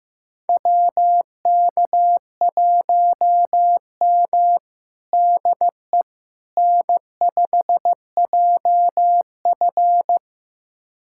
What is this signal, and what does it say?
Morse code, heard as a keyed tone, about 16 wpm, 700 Hz. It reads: WK1M DE N5JF
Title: USA Callsign DE Callsign - 4 Characters - 15wpm